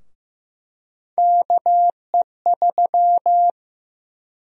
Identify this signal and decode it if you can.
Morse code, heard as a keyed tone, about 15 wpm, 700 Hz. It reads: KE3
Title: USA Call Sign Prefixes - 15wpm